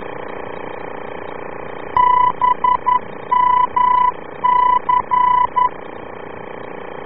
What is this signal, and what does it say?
Morse code, heard as a keyed tone, about 11 wpm, 1000 Hz. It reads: BMC